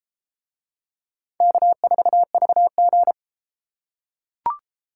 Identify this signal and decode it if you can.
Morse code, heard as a keyed tone, about 32 wpm, 700 Hz. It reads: K4VG